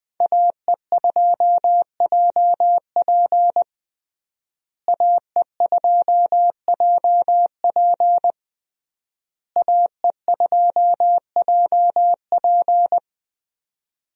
Morse code, heard as a keyed tone, about 21 wpm, 700 Hz. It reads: AE2JP AE2JP AE2JP
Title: USA Call Signs - 5 Characters - Repeated 3x - 20wpm